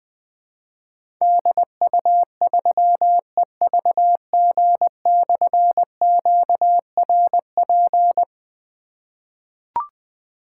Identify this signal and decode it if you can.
Morse code, heard as a keyed tone, about 20 wpm, 700 Hz. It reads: DU3EVG/QRP
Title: Difficult Call Signs 20wpm